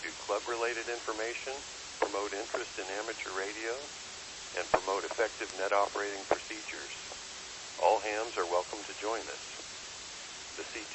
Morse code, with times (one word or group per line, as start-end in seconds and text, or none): none